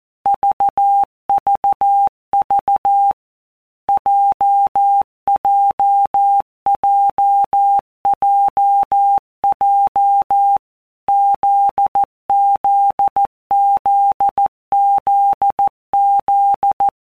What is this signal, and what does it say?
Morse code, heard as a keyed tone, about 14 wpm, 800 Hz. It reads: VVV JJJJJ ZZZZZ
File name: Basic Letter Groups By WPM's / Groups@13 / JZ17@13